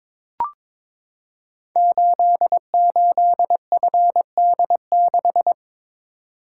Morse code, heard as a keyed tone, about 22 wpm, 700 Hz. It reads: 88FD6
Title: Code Groups - 5 Letter:Numbers - 22wpm